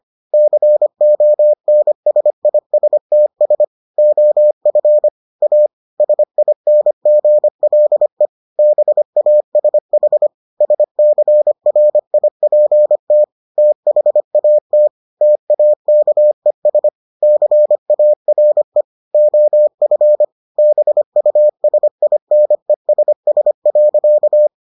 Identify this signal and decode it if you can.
Morse code, heard as a keyed tone, about 25 wpm, 600 Hz. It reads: CONSISTS OF A SINGLE BASH SCRIPT THAT TAKES CARE OF BUSINESS.